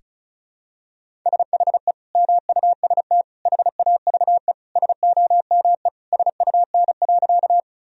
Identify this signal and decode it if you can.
Morse code, heard as a keyed tone, about 36 wpm, 700 Hz. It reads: SHE MUST HAVE SOME SUN.
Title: Sentences from Top 1000 Words - Review Part 1 - 35wpm